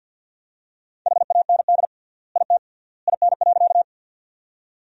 Morse code, heard as a keyed tone, about 52 wpm, 700 Hz. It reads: HAND IT IN.
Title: 3x Word-Spacing - Sentences from Top 200 Words - 50wpm